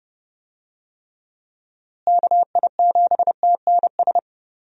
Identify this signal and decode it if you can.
Morse code, heard as a keyed tone, about 30 wpm, 700 Hz. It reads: KI7TNS
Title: USA Call Signs - 6 Characters - Repeated 3x - 30wpm